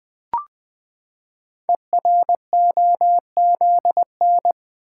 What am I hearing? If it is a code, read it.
Morse code, heard as a keyed tone, about 20 wpm, 700 Hz. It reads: EROZN
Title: Code Groups - 5 Letters - 20wpm